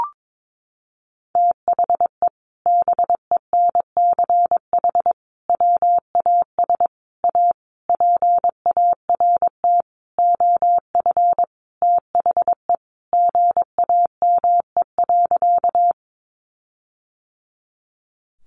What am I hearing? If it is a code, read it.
Morse code, heard as a keyed tone, about 22 wpm, 700 Hz. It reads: THE BENCH WAS A PART OF THE GAME.
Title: Sentences from Top 1400 Words 22wpm